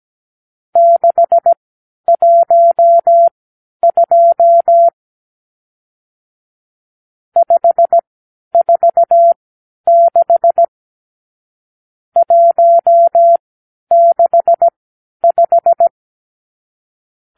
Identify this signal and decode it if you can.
Morse code, heard as a keyed tone, about 17 wpm, spaced as for 10 wpm, 680 Hz. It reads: 612 546 165
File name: k7qo / Lesson 50 (1-6)